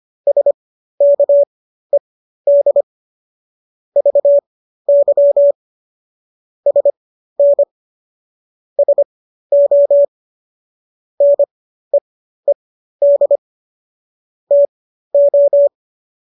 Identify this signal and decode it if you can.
Morse code, heard as a keyed tone, about 25 wpm, spaced as for 13 wpm, 570 Hz. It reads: SKED VY SN SO NEED TO